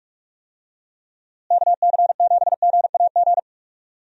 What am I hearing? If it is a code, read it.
Morse code, heard as a keyed tone, about 42 wpm, 700 Hz. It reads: KC7GAG
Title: USA Call Signs - 6 Characters - Repeated 2x - 45wpm